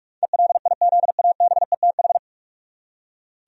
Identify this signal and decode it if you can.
Morse code, heard as a keyed tone, about 46 wpm, 700 Hz. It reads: ELIZABETH
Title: USA Common Names - 45wpm